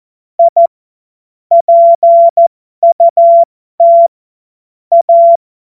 Morse code, heard as a keyed tone, about 14 wpm, spaced as for 12 wpm, 680 Hz. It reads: I PUT A